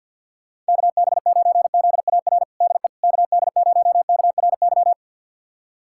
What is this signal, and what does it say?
Morse code, heard as a keyed tone, about 53 wpm, 700 Hz. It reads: KB9ZAR DE KD0KRY
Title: USA Callsign DE Callsign - 6 Characters - 50wpm